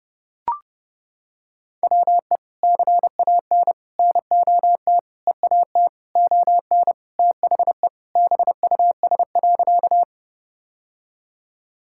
Morse code, heard as a keyed tone, about 30 wpm, 700 Hz. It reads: WE CAN NOT EAT ON THE BUS.
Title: Sentences from Top 1000 Words - Review Part 6 - 30wpm